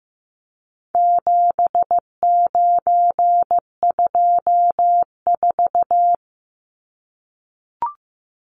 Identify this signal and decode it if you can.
Morse code, heard as a keyed tone, about 15 wpm, 700 Hz. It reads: 7924
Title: Numbers - 4-Digit 15wpm